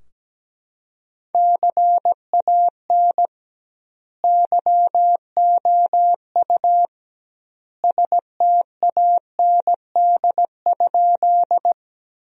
Morse code, heard as a keyed tone, about 17 wpm, 700 Hz. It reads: CAN YOU STAND?
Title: 2x Word-Spacing - Sentences from Top 300 Words - 17wpm